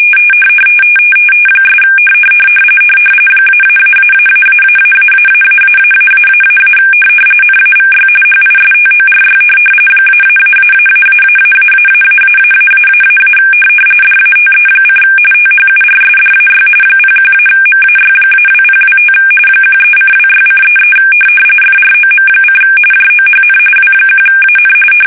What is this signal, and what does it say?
RTTY, about 45.45 baud. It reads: ' -------------------- NAVY HISTORY -------------------- DATE: THURSDAY 01-MAY-2025 13:26:33 FROM: JOHN W1AN
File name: fsk850 / battleships